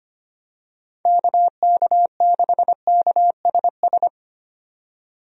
Morse code, heard as a keyed tone, about 25 wpm, 700 Hz. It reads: KK6KSS
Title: USA Call Signs - 6 Characters - 25wpm